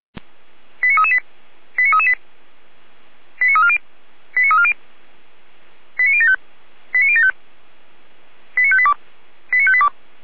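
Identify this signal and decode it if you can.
Five-tone selective call (ZVEI), heard as quick runs of five tones, tones of about 70 ms. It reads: 89338 89338 89340 89340 89075 89075 89582 89582
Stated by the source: ZVEI five-tone selective call received on SDR.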